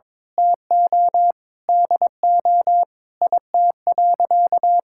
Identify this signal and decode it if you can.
Morse code, heard as a keyed tone, about 22 wpm, 700 Hz. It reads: TO DO IT.
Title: Sentences from Top 1000 Words - Review Part 6 - 22wpm